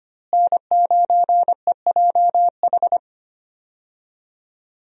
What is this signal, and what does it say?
Morse code, heard as a keyed tone, about 25 wpm, 700 Hz. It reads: N9EJH
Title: USA Call Signs - 5 Characters - Repeated 3x - 25wpm